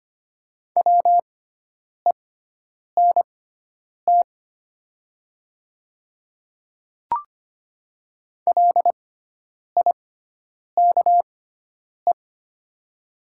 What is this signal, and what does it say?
Morse code, heard as a keyed tone, about 25 wpm, spaced as for 9 wpm, 700 Hz. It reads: WENT LIKE
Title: 6x Character Spacing - Top 200 Words - 25wpm